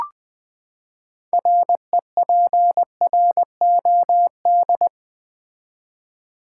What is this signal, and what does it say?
Morse code, heard as a keyed tone, about 20 wpm, 700 Hz. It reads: REPROD
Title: Build A Word - 6 Letters At A Time - 12-Letter Words - 20wpm